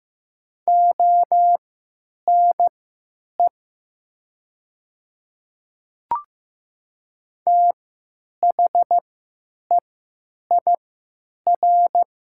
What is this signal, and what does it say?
Morse code, heard as a keyed tone, about 15 wpm, spaced as for 8 wpm, 700 Hz. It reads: ONE THEIR
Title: 3x Character Spacing - Top 100 Words - 15wpm